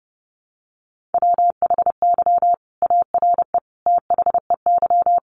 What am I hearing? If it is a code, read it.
Morse code, heard as a keyed tone, about 30 wpm, 700 Hz. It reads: WHY ARE THEY
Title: Sentences from Top 200 Words 30wpm